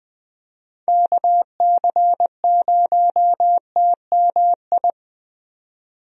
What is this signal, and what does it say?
Morse code, heard as a keyed tone, about 20 wpm, 700 Hz. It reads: KC0TMI
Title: USA Call Signs - 6 Characters - 20wpm